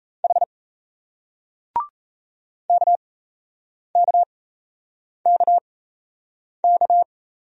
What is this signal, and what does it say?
Morse code, heard as a keyed tone, about 37 wpm, spaced as for 8 wpm, 700 Hz. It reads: H KKKK